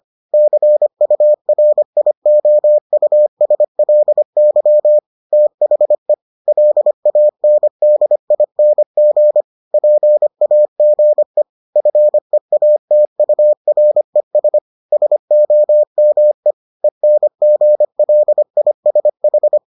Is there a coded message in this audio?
Yes: Morse code from 0.3 to 19.6 s, heard as a keyed tone, about 25 wpm, 600 Hz. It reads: CURIOUSLY THE LANDING PAGE FEATURES SOME ENGLISH